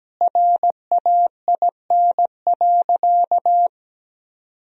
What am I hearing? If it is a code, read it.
Morse code, heard as a keyed tone, about 17 wpm, 700 Hz. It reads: RAIN.
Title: Sentences from Top 1000 Words - Review Part 1 - 17wpm